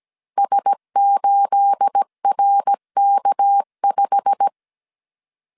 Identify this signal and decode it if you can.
Morse code, heard as a keyed tone, about 17 wpm, 800 Hz. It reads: S8RK5
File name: CW / DET / 1 9 1500C S A 16wpm 2828